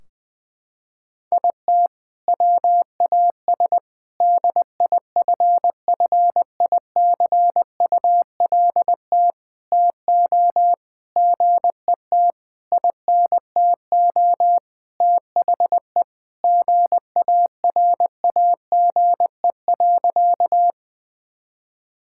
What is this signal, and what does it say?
Morse code, heard as a keyed tone, about 20 wpm, 700 Hz. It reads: IT WAS DIFFICULT TO GET INTO THE GARAGE.
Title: Sentences from Top 1600 Words 20wpm